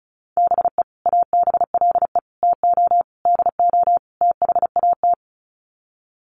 Morse code, heard as a keyed tone, about 34 wpm, 700 Hz. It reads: BE ABLE TO DO THAT
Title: Sets of 5 Words Encore2 35wpm